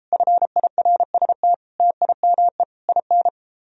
Morse code, heard as a keyed tone, about 33 wpm, 700 Hz. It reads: FIRST TIME IN